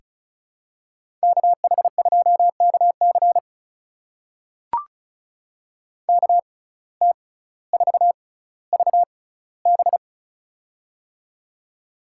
Morse code, heard as a keyed tone, about 36 wpm, spaced as for 18 wpm, 700 Hz. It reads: KH2KC KT4VB
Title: USA Call Signs - 6x Character Spacing - 35wpm